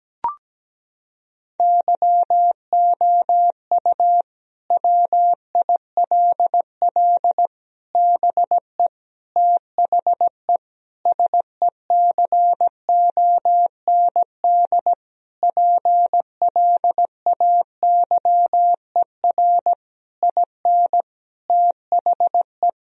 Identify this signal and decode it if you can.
Morse code, heard as a keyed tone, about 17 wpm, 700 Hz. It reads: YOU WILL BE THE SECOND PLAYER IN THE
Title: Sentences from Top 1000 Words - Review Part 4 - 17wpm